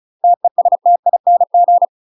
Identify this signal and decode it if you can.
Morse code, heard as a keyed tone, about 36 wpm, 700 Hz. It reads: TESTING
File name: UK MASTER QSOs TRAINING / 35WPM / UK MASTER QSOs 0035